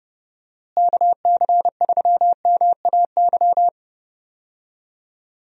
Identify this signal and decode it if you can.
Morse code, heard as a keyed tone, about 31 wpm, 700 Hz. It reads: KC3MAY
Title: USA Call Signs - 6 Characters - Repeated 3x - 30wpm